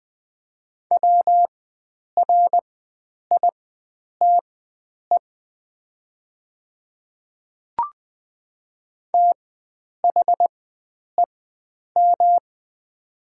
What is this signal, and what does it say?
Morse code, heard as a keyed tone, about 20 wpm, spaced as for 9 wpm, 700 Hz. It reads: WRITE THEM
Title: 4x Character Spacing - Top 100 Words - 20wpm